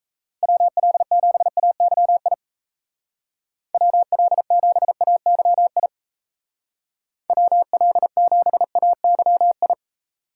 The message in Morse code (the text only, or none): WL7AYI WL7AYI WL7AYI